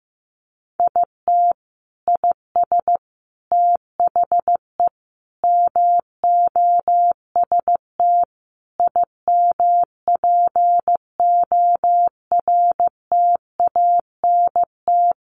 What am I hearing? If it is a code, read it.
Morse code, heard as a keyed tone, about 15 wpm, 700 Hz. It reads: IT IS THE MOST IMPORTANT
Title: Sentences from Top 1500 Words 15wpm